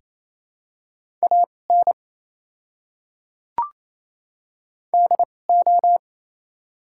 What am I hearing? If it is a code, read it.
Morse code, heard as a keyed tone, about 27 wpm, spaced as for 5 wpm, 690 Hz. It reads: AND O
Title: 2x Character Spacing - Top 100 Words - 28wpm